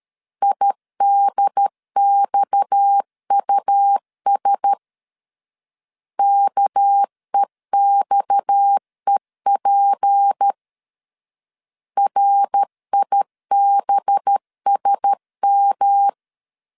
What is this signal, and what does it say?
Morse code, heard as a keyed tone, about 13 wpm, 800 Hz. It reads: IDXUS KEXEP RIBSM
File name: CW / DET / 1 6 1500C S B 12wpm 3431